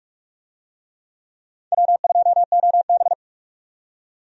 Morse code, heard as a keyed tone, about 46 wpm, 700 Hz. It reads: W2OB
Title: USA Call Signs - 4 Characters - 45wpm